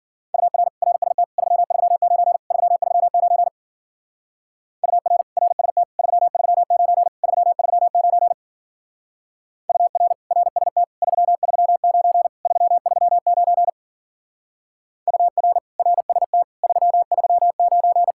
Morse code, heard as a keyed tone, about 52 wpm, 700 Hz. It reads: UR RST 339 339 UR RST 339 339 UR RST 339 339 UR RST 339